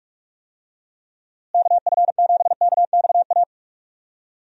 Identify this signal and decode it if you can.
Morse code, heard as a keyed tone, about 46 wpm, 700 Hz. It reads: KF7KXA
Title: USA Call Signs - 6 Characters - Repeated 3x - 45wpm